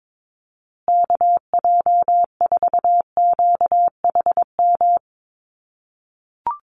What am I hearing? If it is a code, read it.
Morse code, heard as a keyed tone, about 22 wpm, 700 Hz. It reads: KJ4QHM
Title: USA Call Signs Repeated 2x 22wpm